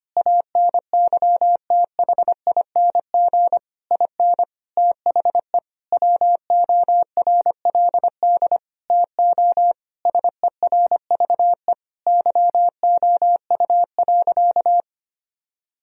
Morse code, heard as a keyed tone, about 25 wpm, 700 Hz. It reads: ANYTHING IN THE WORLD TO SERVE YOU.
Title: Sentences from Top 1000 Words - Review Part 2 - 25wpm